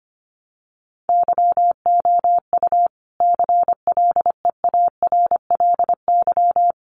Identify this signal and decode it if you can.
Morse code, heard as a keyed tone, about 25 wpm, 700 Hz. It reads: YOU CLEARLY
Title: Sentences from Top 1300 Words 25wpm